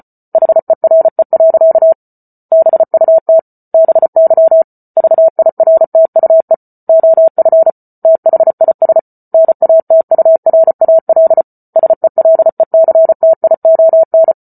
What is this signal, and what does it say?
Morse code, heard as a keyed tone, about 34 wpm, 650 Hz. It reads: HERE. BUT BY VIRTUE OF THIS NATURAL SELECTION
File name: Book Two / 08-Chapter 8 - 34 WPM